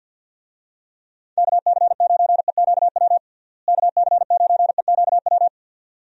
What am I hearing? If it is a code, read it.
Morse code, heard as a keyed tone, about 52 wpm, 700 Hz. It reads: KC9EQW KC9EQW
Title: USA Call Signs - 6 Characters - Repeated 2x - 50wpm